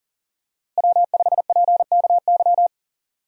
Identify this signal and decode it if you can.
Morse code, heard as a keyed tone, about 40 wpm, 700 Hz. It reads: W5PKY
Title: USA Call Signs - 5 Characters - Repeated 3x - 40wpm